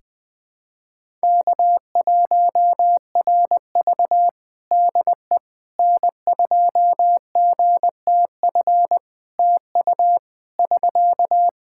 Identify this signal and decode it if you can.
Morse code, heard as a keyed tone, about 20 wpm, 700 Hz. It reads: K1RV DE N2GTF TU <SK>
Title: Practice QSO - Exchange 4 - Ending - 20wpm